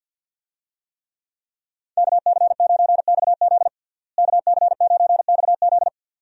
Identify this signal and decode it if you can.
Morse code, heard as a keyed tone, about 50 wpm, 700 Hz. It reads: KC9XZ KC9XZ
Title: USA Call Signs - 5 Characters - Repeated 2x - 50wpm